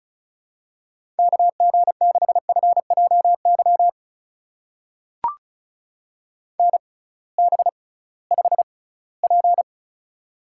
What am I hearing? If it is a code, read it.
Morse code, heard as a keyed tone, about 36 wpm, spaced as for 20 wpm, 700 Hz. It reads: KG6FJY NB5P
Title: USA Call Signs - 6x Character Spacing - 35wpm